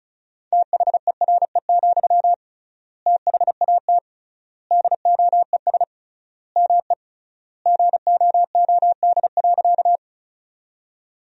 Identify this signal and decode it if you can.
Morse code, heard as a keyed tone, about 36 wpm, 700 Hz. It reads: THERE, THAT DOES ME GOOD.
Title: 3x Word-Spacing - Sentences from Top 200 Words - 35wpm